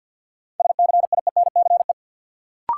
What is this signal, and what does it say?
Morse code, heard as a keyed tone, about 52 wpm, 700 Hz. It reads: SCIENCE
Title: Sentences from Top 1400 Words - New Word Drill Alt - 50wpm